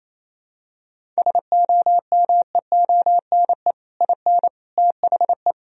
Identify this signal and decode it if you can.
Morse code, heard as a keyed tone, about 28 wpm, 700 Hz. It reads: SOMEONE IN THE
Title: Sentences from Top 1000 Words - Review Part 1 - 28wpm